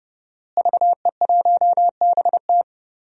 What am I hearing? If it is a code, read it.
Morse code, heard as a keyed tone, about 30 wpm, 700 Hz. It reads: VE1BT
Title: DXCC-100 Call Signs - 5 Characters - Repeated 2x - 30wpm